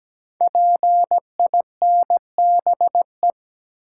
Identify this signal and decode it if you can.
Morse code, heard as a keyed tone, about 17 wpm, 700 Hz. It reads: PINBE
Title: Code Groups - 5 Letter:Numbers - 17wpm